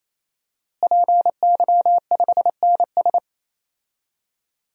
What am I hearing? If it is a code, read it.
Morse code, heard as a keyed tone, about 27 wpm, 700 Hz. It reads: PY5NS
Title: DXCC-100 Call Signs - 5 Characters - Repeated 3x - 28wpm